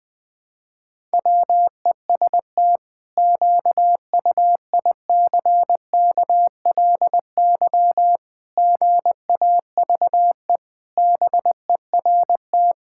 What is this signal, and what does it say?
Morse code, heard as a keyed tone, about 21 wpm, 700 Hz. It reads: WEST QUICKLY GAVE BERT
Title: Pangram Challenge v2 - 20wpm